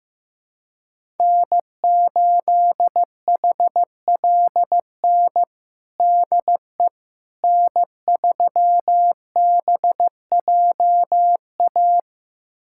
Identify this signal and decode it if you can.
Morse code, heard as a keyed tone, about 16 wpm, 700 Hz. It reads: N8HLN DE N3BJA
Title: USA Callsign DE Callsign - 5 Characters - 15wpm